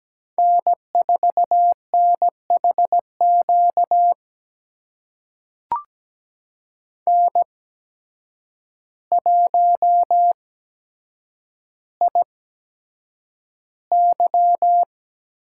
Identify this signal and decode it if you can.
Morse code, heard as a keyed tone, about 17 wpm, spaced as for 8 wpm, 700 Hz. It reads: N4NHQ N1IY
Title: USA Call Signs - 5 Characters - 8x Character Spacing - 17wpm